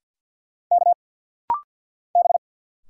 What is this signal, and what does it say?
Morse code, heard as a keyed tone, about 50 wpm, 700 Hz. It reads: K B
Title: Single Letter-Number Alt - Rapid-Fire 50wpm